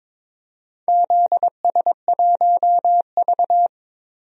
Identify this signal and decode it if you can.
Morse code, heard as a keyed tone, about 22 wpm, 700 Hz. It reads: ZS1V
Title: DXCC-100 Call Signs - 4 Characters - 22wpm